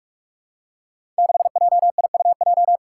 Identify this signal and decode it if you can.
Morse code, heard as a keyed tone, about 47 wpm, 700 Hz. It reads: 6JIUJ
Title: Five Challenging Letter-Number Combinations 45wpm